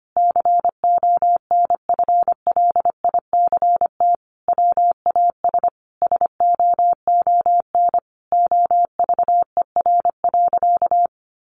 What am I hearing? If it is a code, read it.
Morse code, heard as a keyed tone, about 25 wpm, 700 Hz. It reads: CONFLICT WAS SOON OVER.
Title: Sentences from Top 1300 Words 25wpm